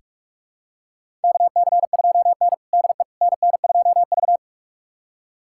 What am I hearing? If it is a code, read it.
Morse code, heard as a keyed tone, about 46 wpm, 700 Hz. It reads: KC2N DE NN2V
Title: USA Callsign DE Callsign - 4 Characters - 45wpm